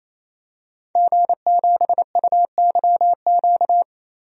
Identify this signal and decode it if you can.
Morse code, heard as a keyed tone, about 28 wpm, 700 Hz. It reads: G7UYQ
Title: DXCC-100 Call Signs - 5 Characters - 28wpm